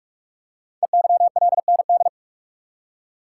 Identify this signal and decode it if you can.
Morse code, heard as a keyed tone, about 44 wpm, 700 Hz. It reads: EYLND
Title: Code Groups - 5 Letters - 45wpm